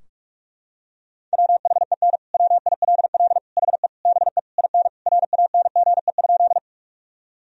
Morse code, heard as a keyed tone, about 46 wpm, 700 Hz. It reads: WHEN WILL HE BE IN RANGE?
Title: Sentences from Top 1000 Words - Review Part 4 - 45wpm